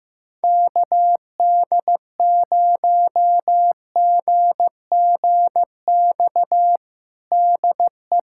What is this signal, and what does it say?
Morse code, heard as a keyed tone, about 15 wpm, 700 Hz. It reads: KD0GGX DE
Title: USA Callsign DE Callsign 15wpm